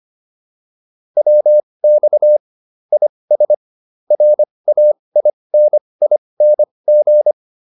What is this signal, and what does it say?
Morse code, heard as a keyed tone, about 25 wpm, spaced as for 20 wpm, 600 Hz. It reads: WX IS RAINING
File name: QSO402 20